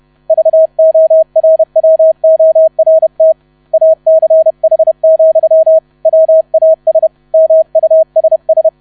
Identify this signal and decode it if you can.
Morse code, heard as a keyed tone, about 31 wpm, 640 Hz. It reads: VORWORT ACH, WAS MUSS
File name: 30wpm